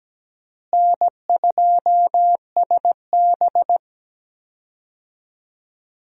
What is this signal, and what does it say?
Morse code, heard as a keyed tone, about 17 wpm, 700 Hz. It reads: N2SB
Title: USA Call Signs - 4 Characters - Repeated 2x - 17wpm